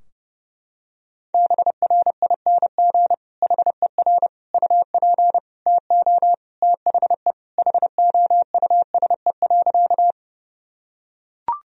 Morse code, heard as a keyed tone, about 30 wpm, 700 Hz. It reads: BRING HER UP TO THE HOUSE.
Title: Sentences from Top 1000 Words - Review Part 3 - 30wpm